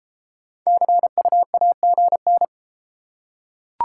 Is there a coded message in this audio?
Yes: Morse code from 0.7 to 2.4 s, heard as a keyed tone, about 33 wpm, 700 Hz. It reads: CUAGN